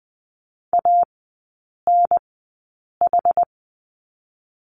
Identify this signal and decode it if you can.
Morse code, heard as a keyed tone, about 20 wpm, spaced as for 8 wpm, 690 Hz. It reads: ANH
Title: 060 Lesson 11e - Letter h - Three Characters 20wpm